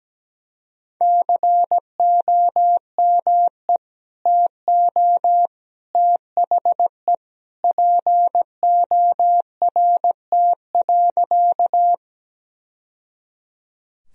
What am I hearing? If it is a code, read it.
Morse code, heard as a keyed tone, about 17 wpm, 700 Hz. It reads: COME TO THE PORT.
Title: Sentences from Top 1000 Words - Review Part 1 - 17wpm